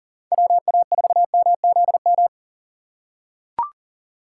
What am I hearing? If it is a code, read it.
Morse code, heard as a keyed tone, about 40 wpm, 700 Hz. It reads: WA4MZM
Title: USA Call Signs - 6 Characters - 2x Character Spacing - 40wpm